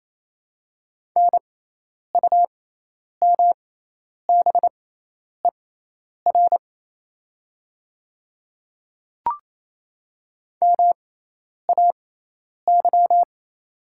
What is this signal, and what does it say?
Morse code, heard as a keyed tone, about 28 wpm, spaced as for 10 wpm, 700 Hz. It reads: NUMBER MAY
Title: 6x Character Spacing - Top 100 Words - 28wpm